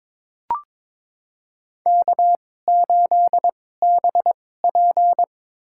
Morse code, heard as a keyed tone, about 22 wpm, spaced as for 16 wpm, 700 Hz. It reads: K8BP
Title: USA Call Signs - 4 Characters - 2x Character Spacing - 22wpm